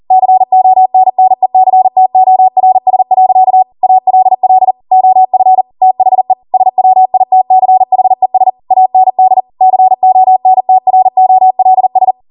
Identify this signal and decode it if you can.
Morse code, heard as a keyed tone, about 40 wpm, 750 Hz. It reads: CONNECTORS. ALL OF THE SWITCHES AND CONTROLS
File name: Morse / 40WPM